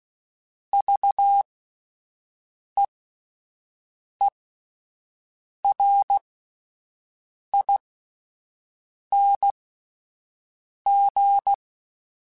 Morse code, heard as a keyed tone, about 16 wpm, spaced as for 5 wpm, 800 Hz. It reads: VEERING